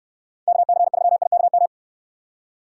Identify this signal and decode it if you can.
Morse code, heard as a keyed tone, about 69 wpm, 700 Hz. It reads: KC3IKM